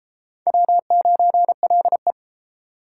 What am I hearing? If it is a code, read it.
Morse code, heard as a keyed tone, about 31 wpm, 700 Hz. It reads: W9LE